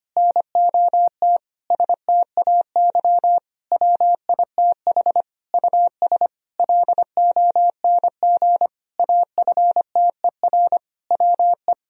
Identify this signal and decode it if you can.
Morse code, heard as a keyed tone, about 25 wpm, 700 Hz. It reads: NOT STAY WITH US LONG AFTER WE
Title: Sentences from Top 1000 Words - Review Part 2 - 25wpm